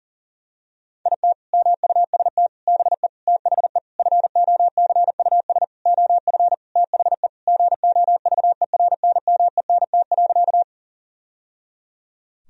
Morse code, heard as a keyed tone, about 40 wpm, 700 Hz. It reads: IT MUST BE THE FOCUS OF THE GOVERNMENT.
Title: Sentences from Top 1000 Words - Review Part 4 - 40wpm